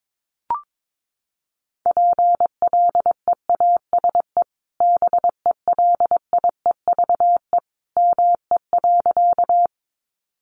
Morse code, heard as a keyed tone, about 22 wpm, 700 Hz. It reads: PLEASE BELIEVE ME.
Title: Sentences from Top 1500 Words 22wpm